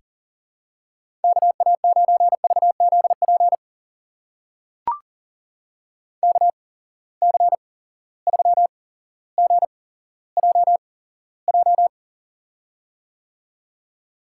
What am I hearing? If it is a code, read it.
Morse code, heard as a keyed tone, about 43 wpm, spaced as for 17 wpm, 700 Hz. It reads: KA9VZP KC3GJJ
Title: USA Call Signs - 6 Characters - 8x Character Spacing - 40wpm